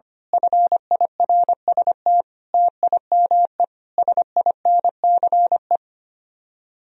Morse code, heard as a keyed tone, about 24 wpm, 700 Hz. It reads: FIRST TIME SINCE
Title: Speed-Racing - Sets of 5 Words 25wpm